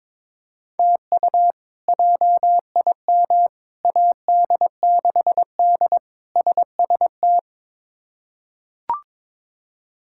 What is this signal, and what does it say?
Morse code, heard as a keyed tone, about 22 wpm, 700 Hz. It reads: TU JIM AD6D SST
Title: K1USN SST QSO Practice - 22wpm